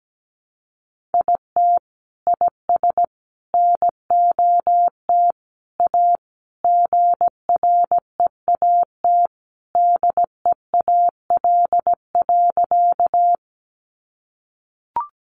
Morse code, heard as a keyed tone, about 17 wpm, 700 Hz. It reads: IT IS NOT A GREAT DEAL.
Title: Sentences from Top 800 Words 17wpm